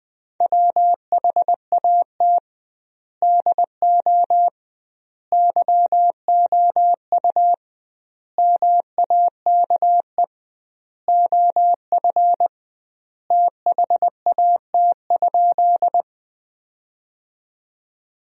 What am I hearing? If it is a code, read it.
Morse code, heard as a keyed tone, about 20 wpm, 700 Hz. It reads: WHAT DO YOU MAKE OF THAT?
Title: 2x Word-Spacing - Sentences from Top 100 Words - 20wpm